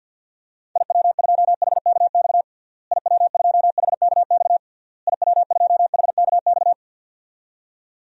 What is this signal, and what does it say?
Morse code, heard as a keyed tone, about 47 wpm, 700 Hz. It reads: IW2HKX IW2HKX IW2HKX
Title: DXCC-100 Call Signs - 6 Characters - Repeated 3x - 50wpm